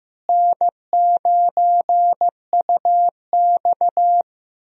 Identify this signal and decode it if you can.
Morse code, heard as a keyed tone, about 15 wpm, 700 Hz. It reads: N9UX
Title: USA Call Signs - 4 Characters - Repeated 3x - 15wpm